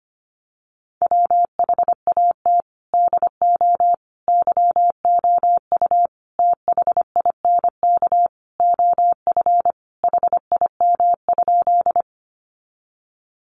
Morse code, heard as a keyed tone, about 25 wpm, 700 Hz. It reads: WHAT DO YOU THINK OF HIM?
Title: Sentences from Top 200 Words 25wpm